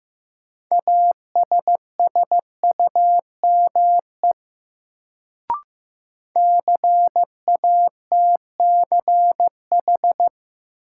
Morse code, heard as a keyed tone, about 16 wpm, 700 Hz. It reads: ASSUME CATCH
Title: Sentences from Top 1500 Words - New Word Drill Alt - 15wpm